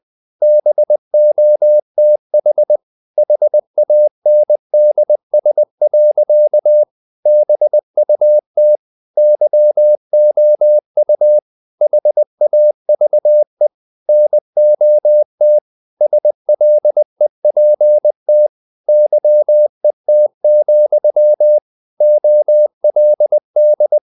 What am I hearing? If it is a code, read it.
Morse code, heard as a keyed tone, about 20 wpm, 600 Hz. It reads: BOTH HANDS. BUT YOU HAVE NOT SLEPT YET, OLD
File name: the old man and the sea part 2 20wpm